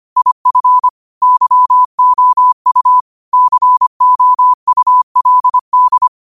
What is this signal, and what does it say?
Morse code, heard as a keyed tone, about 25 wpm, 1000 Hz. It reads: IF YOU COULD
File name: qotdfiles / 25wpm